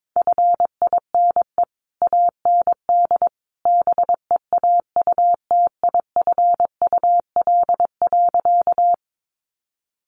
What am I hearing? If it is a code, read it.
Morse code, heard as a keyed tone, about 22 wpm, 700 Hz. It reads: FINE AND BEAUTIFUL.
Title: Sentences from Top 900 Words 22wpm